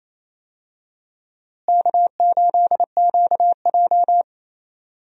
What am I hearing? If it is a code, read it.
Morse code, heard as a keyed tone, about 28 wpm, 700 Hz. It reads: K8QJ
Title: USA Call Signs - 4 Characters - Repeated 3x - 28wpm